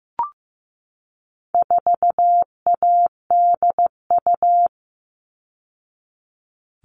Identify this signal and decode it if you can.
Morse code, heard as a keyed tone, about 15 wpm, 700 Hz. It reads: 4ADU
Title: All Cut Numbers - 4-Digit 15wpm